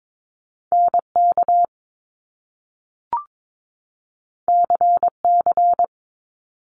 Morse code, heard as a keyed tone, about 22 wpm, 700 Hz. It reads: NK CC
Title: Top Two Letter Combinations in Words 22wpm